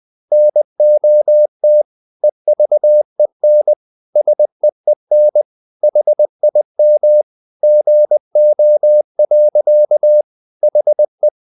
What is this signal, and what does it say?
Morse code, heard as a keyed tone, about 20 wpm, 600 Hz. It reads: NOT EVEN SEEN HIM GO. HE
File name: the old man and the sea / the old man and the sea part 1 20wpm